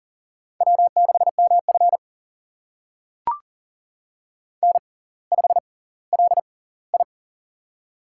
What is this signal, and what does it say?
Morse code, heard as a keyed tone, about 39 wpm, spaced as for 20 wpm, 700 Hz. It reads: W6MF N5LI
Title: USA Call Signs - 4 Characters - 6x Character Spacing - 40wpm